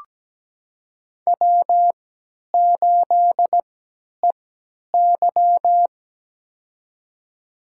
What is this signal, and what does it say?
Morse code, heard as a keyed tone, about 17 wpm, spaced as for 10 wpm, 700 Hz. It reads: W8EY
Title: USA Call Signs - 4 Characters - 3x Character Spacing - 17wpm